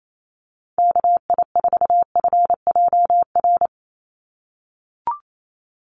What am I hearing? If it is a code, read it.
Morse code, heard as a keyed tone, about 28 wpm, 700 Hz. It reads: KI4FJR
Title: USA Call Signs 28wpm